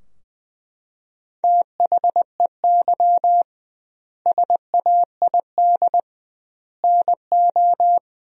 Morse code, heard as a keyed tone, about 20 wpm, 700 Hz. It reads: THEY SAID NO
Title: 2x Word-Spacing - Sentences from Top 100 Words - 20wpm